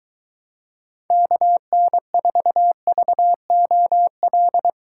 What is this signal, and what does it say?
Morse code, heard as a keyed tone, about 23 wpm, 700 Hz. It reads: KN4VOL